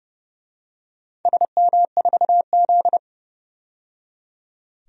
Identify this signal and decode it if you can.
Morse code, heard as a keyed tone, about 30 wpm, 700 Hz. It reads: SM4Z
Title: DXCC-100 Call Signs - 4 Characters - 30wpm